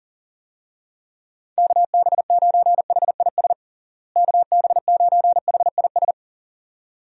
Morse code, heard as a keyed tone, about 40 wpm, 700 Hz. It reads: KB9HIS KB9HIS
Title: USA Call Signs - 6 Characters - Repeated 2x - 40wpm